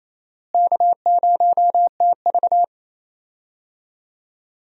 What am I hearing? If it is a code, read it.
Morse code, heard as a keyed tone, about 28 wpm, 700 Hz. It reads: K0TV
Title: USA Call Signs - 4 Characters - Repeated 3x - 28wpm